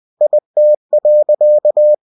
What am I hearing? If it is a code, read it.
Morse code, heard as a keyed tone, about 20 wpm, 600 Hz. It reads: IT.
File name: cw-books / winnie-the-pooh / winnie-20wpm-page-0106